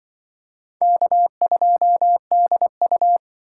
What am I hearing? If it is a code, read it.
Morse code, heard as a keyed tone, about 25 wpm, 700 Hz. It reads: K2DU